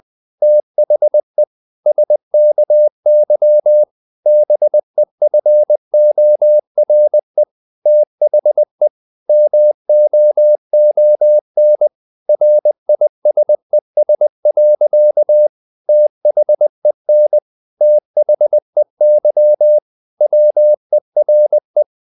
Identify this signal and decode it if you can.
Morse code, heard as a keyed tone, about 20 wpm, 600 Hz. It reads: THE SKY BEFORE THE MOON RISES. THEN THEY WERE
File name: the old man and the sea part 2 20wpm